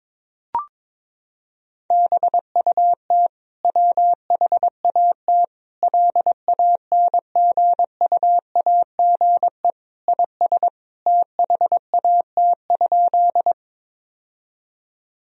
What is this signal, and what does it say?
Morse code, heard as a keyed tone, about 22 wpm, 700 Hz. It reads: BUT WHAT LANGUAGE IS THAT?
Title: Sentences from Top 1000 Words - Review Part 3 - 22wpm